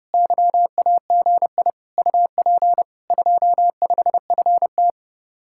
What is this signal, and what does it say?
Morse code, heard as a keyed tone, about 40 wpm, 700 Hz. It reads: YAGI UP 25FT